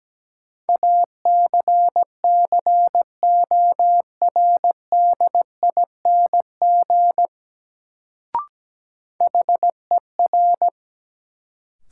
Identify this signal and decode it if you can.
Morse code, heard as a keyed tone, about 17 wpm, 700 Hz. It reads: ACCORDING HER
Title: Sets of 2 Words - New Word Drill - 17wpm